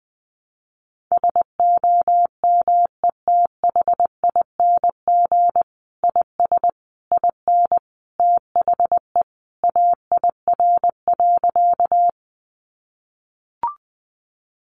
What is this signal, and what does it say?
Morse code, heard as a keyed tone, about 20 wpm, 700 Hz. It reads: SOMETHING IS IN THE AIR.
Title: Sentences from Top 600 Words 20wpm